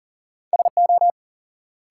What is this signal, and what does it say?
Morse code, heard as a keyed tone, about 40 wpm, 700 Hz. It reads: SO
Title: Sentences from Top 100 Words - New Word Drill - 40wpm